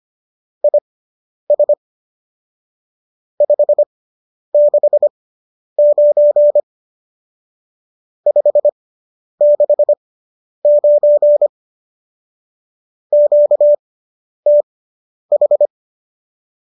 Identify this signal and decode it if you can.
Morse code, heard as a keyed tone, about 25 wpm, spaced as for 10 wpm, 590 Hz. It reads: IS 569 569 QTH
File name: qso005 10